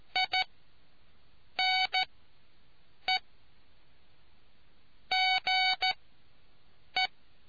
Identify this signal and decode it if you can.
Morse code, heard as a keyed tone, about 14 wpm, spaced as for 6 wpm, 750 Hz. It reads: I NE GE